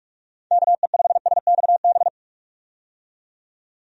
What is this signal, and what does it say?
Morse code, heard as a keyed tone, about 44 wpm, 700 Hz. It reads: KE5SXB
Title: USA Call Signs - 6 Characters - 45wpm